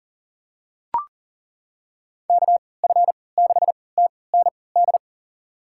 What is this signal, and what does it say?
Morse code, heard as a keyed tone, about 40 wpm, spaced as for 23 wpm, 700 Hz. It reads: KF6TND
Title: USA Call Signs - 6 Characters - 3x Character Spacing - 40wpm